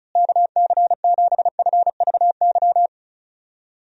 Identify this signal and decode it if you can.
Morse code, heard as a keyed tone, about 36 wpm, 700 Hz. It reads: KC7FVY
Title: USA Call Signs - 6 Characters - Repeated 2x - 35wpm